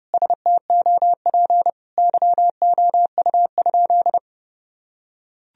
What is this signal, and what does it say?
Morse code, heard as a keyed tone, about 30 wpm, 700 Hz. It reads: STOP YOU?
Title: Sentences from Top 1000 Words - Review Part 1 - 30wpm